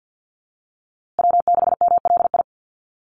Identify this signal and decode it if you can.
Morse code, heard as a keyed tone, about 50 wpm, 700 Hz. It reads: W6MLI
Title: USA Call Signs Repeated 2x 50wpm